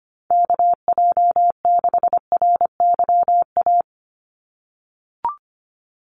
Morse code, heard as a keyed tone, about 25 wpm, 700 Hz. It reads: KJ6RYA
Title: USA Call Signs 25wpm